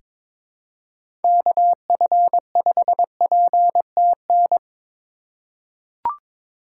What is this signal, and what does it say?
Morse code, heard as a keyed tone, about 21 wpm, 700 Hz. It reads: KF5PTN
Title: USA Call Signs - 6 Characters - 3x Character Spacing - 22wpm